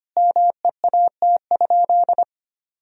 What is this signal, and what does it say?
Morse code, heard as a keyed tone, about 25 wpm, 700 Hz. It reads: MEAT?
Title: Sentences from Top 1000 Words - Review Part 3 - 25wpm